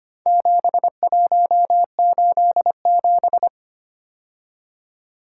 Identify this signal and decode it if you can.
Morse code, heard as a keyed tone, about 25 wpm, 700 Hz. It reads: 7187
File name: practice / 25wpm / Four Letter-Number Combinations - 25wpm